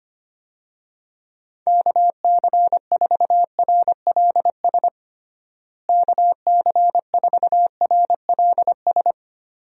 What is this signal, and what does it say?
Morse code, heard as a keyed tone, about 25 wpm, 700 Hz. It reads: KC4RLS KC4RLS
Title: USA Call Signs - 6 Characters - Repeated 2x - 25wpm